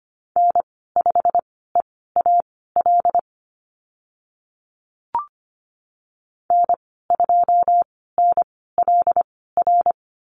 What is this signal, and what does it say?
Morse code, heard as a keyed tone, about 25 wpm, spaced as for 16 wpm, 700 Hz. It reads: N5EAL N2NLR
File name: Lesson 15g - Number 5 - Call Signs 25wpm